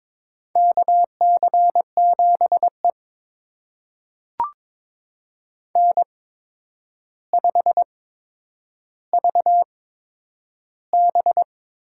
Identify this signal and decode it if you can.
Morse code, heard as a keyed tone, about 22 wpm, spaced as for 9 wpm, 700 Hz. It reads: KC7E N5VB
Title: USA Call Signs - 4 Characters - 8x Character Spacing - 22wpm